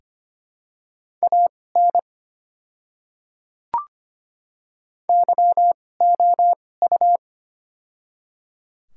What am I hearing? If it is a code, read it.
Morse code, heard as a keyed tone, about 25 wpm, spaced as for 18 wpm, 700 Hz. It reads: AN YOU
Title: 2x Character Spacing - Top 100 Words - 25wpm